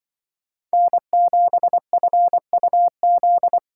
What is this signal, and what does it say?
Morse code, heard as a keyed tone, about 24 wpm, 700 Hz. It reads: N7FUZ
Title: Speed Racing - USA Call Signs 20wpm